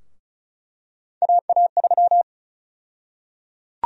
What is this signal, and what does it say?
Morse code, heard as a keyed tone, about 36 wpm, 700 Hz. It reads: AA3
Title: USA Call Sign Prefixes - Repeated 2x - 35wpm